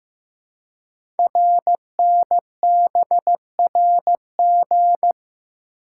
Morse code, heard as a keyed tone, about 15 wpm, 690 Hz. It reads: RNBRG
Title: Code Groups - 5 Letters - 15wpm